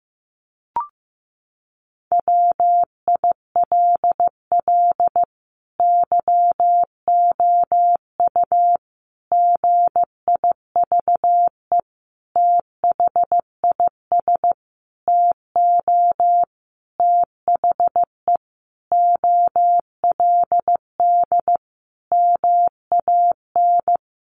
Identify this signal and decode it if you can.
Morse code, heard as a keyed tone, about 15 wpm, 700 Hz. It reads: WILL YOU GIVE THIS TO THE OLD MAN
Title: Sentences from Top 200 Words 15wpm